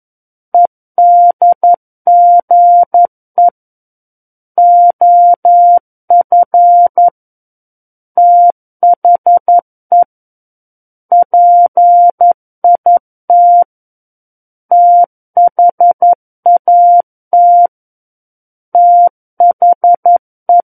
Morse code, heard as a keyed tone, about 11 wpm, 700 Hz. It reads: EDGE OF THE PIT THAT THE
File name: 02-Chapter 2 - 11 WPM